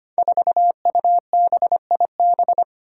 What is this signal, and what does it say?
Morse code, heard as a keyed tone, about 25 wpm, 700 Hz. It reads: 4UBIB
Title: Five Challenging Letter-Number Combinations 25wpm